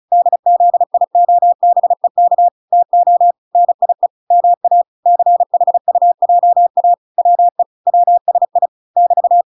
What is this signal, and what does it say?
Morse code, heard as a keyed tone, about 34 wpm, 700 Hz. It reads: DZIOBEK TO NIE MA CHUJA WE WSI <BT>